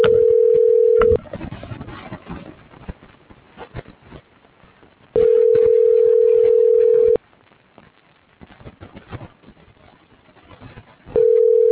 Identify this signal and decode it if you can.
Ringback tone (North America), 440 + 480 Hz.